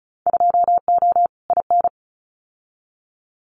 Morse code, heard as a keyed tone, about 34 wpm, 700 Hz. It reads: 2O IN
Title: Winter Field Day Exchanges 35wpm